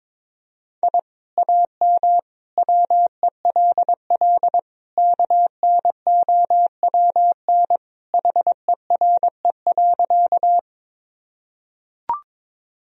Morse code, heard as a keyed tone, about 22 wpm, 700 Hz. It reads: I AM WELL KNOWN HERE.
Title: Sentences from Top 1000 Words - Review Part 3 - 22wpm